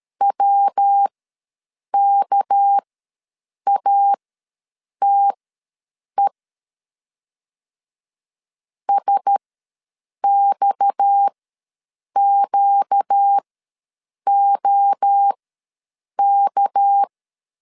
Morse code, heard as a keyed tone, about 13 wpm, spaced as for 7 wpm, 800 Hz. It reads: WKATE SXQOK
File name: LEZIONI / 6 PKX0 / 6 PKX ws 12wpm 3106